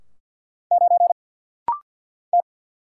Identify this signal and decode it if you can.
Morse code, heard as a keyed tone, about 49 wpm, 700 Hz. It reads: 9 T
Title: Single Letter-Number - Rapid-Fire 50wpm